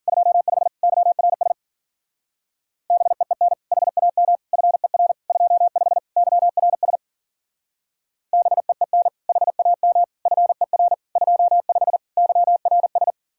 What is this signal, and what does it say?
Morse code, heard as a keyed tone, about 49 wpm, 700 Hz. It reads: 25 YRS BEEN HAM FER 25 YRS BEEN HAM FER 25 YRS
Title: Speed-Racing Common QSO Phrases Encore - 40wpm